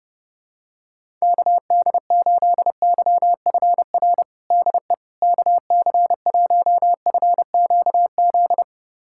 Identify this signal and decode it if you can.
Morse code, heard as a keyed tone, about 30 wpm, 700 Hz. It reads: KD8YFR DE KC1FQZ
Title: USA Callsign DE Callsign 30wpm